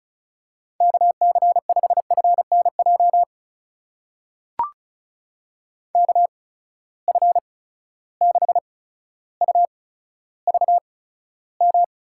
Morse code, heard as a keyed tone, about 36 wpm, spaced as for 15 wpm, 700 Hz. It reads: KC5FNJ KF6UVM
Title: USA Call Signs - 6 Characters - 8x Character Spacing - 35wpm